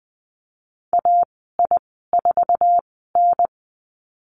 Morse code, heard as a keyed tone, about 20 wpm, spaced as for 15 wpm, 700 Hz. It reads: AI4N
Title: 122 Lesson 20g - Number 3 - Call Signs 20wpm